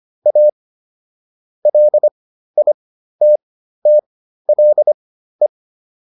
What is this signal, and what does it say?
Morse code, heard as a keyed tone, about 25 wpm, spaced as for 13 wpm, 600 Hz. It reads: A LITTLE
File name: PR405 13